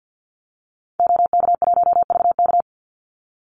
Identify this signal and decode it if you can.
Morse code, heard as a keyed tone, about 48 wpm, 700 Hz. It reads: OK1VK
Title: Call Signs - 50wpm